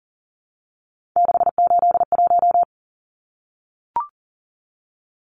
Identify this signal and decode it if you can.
Morse code, heard as a keyed tone, about 40 wpm, 700 Hz. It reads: 681
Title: Numbers - 3-Digit 40wpm